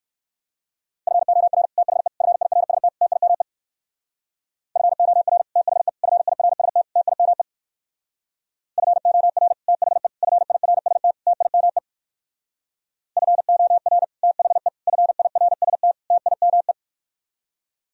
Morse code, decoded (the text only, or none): FOR THE FIRST TIME FOR THE FIRST TIME FOR THE FIRST TIME FOR THE FIRST TIME